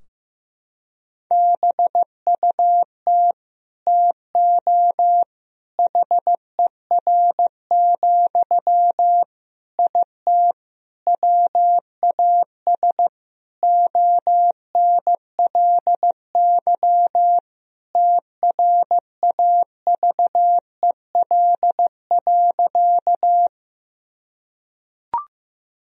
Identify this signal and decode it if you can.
Morse code, heard as a keyed tone, about 15 wpm, 700 Hz. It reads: BUT TO HER, IT WAS ONLY TRAVEL.
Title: Sentences from Top 1000 Words - Review Part 2 - 15wpm